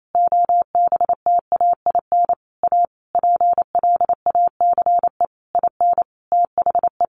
Morse code, heard as a keyed tone, about 28 wpm, 700 Hz. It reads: OBTAIN A PLACE IN THE
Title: Sentences from Top 1300 Words 28wpm